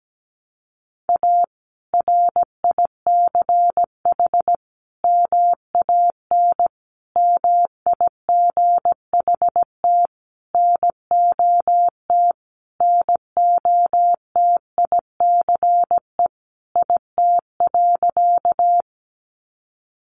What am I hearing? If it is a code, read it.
Morse code, heard as a keyed tone, about 17 wpm, 700 Hz. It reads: A RICH MAN MIGHT NOT NOTICE IT.
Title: Sentences from Top 1100 Words 17wpm